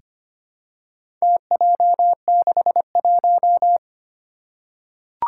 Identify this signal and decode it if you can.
Morse code, heard as a keyed tone, about 25 wpm, 700 Hz. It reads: TJ61
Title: Four Challenging Letter-Number Combinations 25wpm